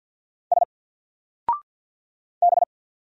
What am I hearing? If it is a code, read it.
Morse code, heard as a keyed tone, about 48 wpm, 700 Hz. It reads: S B
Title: Single Letter-Number Alt 50wpm